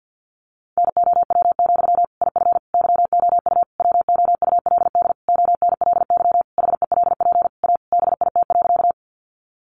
Morse code, heard as a keyed tone, about 49 wpm, 700 Hz. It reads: NOW, IF YOU WOULD ONLY HELP A BIT.
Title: Sentences from Top 900 Words 50wpm